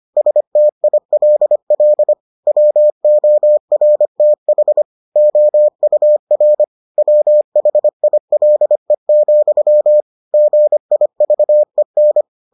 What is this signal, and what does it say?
Morse code, heard as a keyed tone, about 25 wpm, 600 Hz. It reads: STILL WORTH OUR WHILE, GIVEN